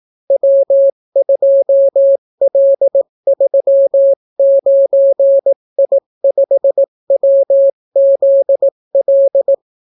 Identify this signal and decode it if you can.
Morse code, heard as a keyed tone, about 18 wpm, 550 Hz. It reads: W2L39I5WZL